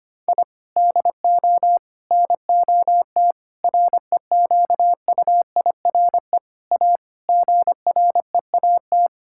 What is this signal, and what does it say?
Morse code, heard as a keyed tone, about 25 wpm, 700 Hz. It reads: I DO NOT REQUIRE A GREAT
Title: Sentences from Top 1000 Words - Review Part 6 - 25wpm